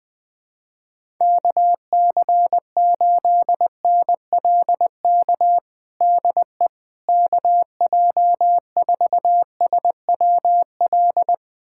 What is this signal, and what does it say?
Morse code, heard as a keyed tone, about 20 wpm, 700 Hz. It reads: KC8NLK DE KJ4SWL
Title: USA Callsign DE Callsign 20wpm